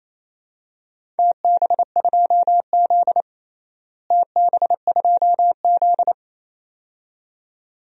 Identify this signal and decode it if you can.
Morse code, heard as a keyed tone, about 28 wpm, 700 Hz. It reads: TB2Z TB2Z
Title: DXCC-100 Call Signs - 4 Characters - Repeated 2x - 28wpm